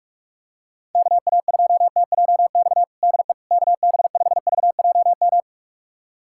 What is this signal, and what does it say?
Morse code, heard as a keyed tone, about 46 wpm, 700 Hz. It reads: KA2TJX DE KB5VJM
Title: USA Callsign DE Callsign 45wpm